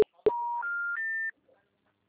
Special information tone.